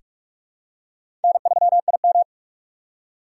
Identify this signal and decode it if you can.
Morse code, heard as a keyed tone, about 42 wpm, 700 Hz. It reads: N3IM
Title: USA Call Signs - 4 Characters - 6x Character Spacing - 45wpm